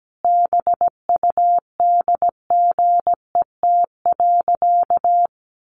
Morse code, heard as a keyed tone, about 17 wpm, 700 Hz. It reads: BUDGET.
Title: Sentences from Top 1200 Words 17wpm